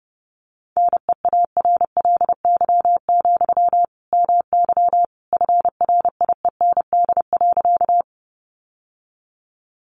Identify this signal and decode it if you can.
Morse code, heard as a keyed tone, about 30 wpm, 700 Hz. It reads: NEARLY, MY FRIEND.
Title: Sentences from Top 1300 Words 30wpm